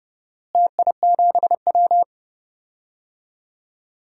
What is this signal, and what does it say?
Morse code, heard as a keyed tone, about 30 wpm, 700 Hz. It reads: TI7W
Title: DXCC-100 Call Signs - 4 Characters - Repeated 3x - 30wpm